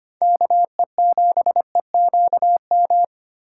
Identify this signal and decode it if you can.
Morse code, heard as a keyed tone, about 33 wpm, 700 Hz. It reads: KE7EQM